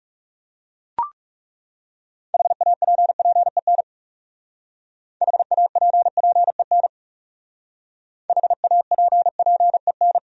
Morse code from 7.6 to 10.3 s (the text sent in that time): HAPPEN